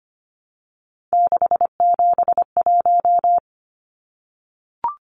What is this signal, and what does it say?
Morse code, heard as a keyed tone, about 25 wpm, 700 Hz. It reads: 671
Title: Numbers - 3-Digit 25wpm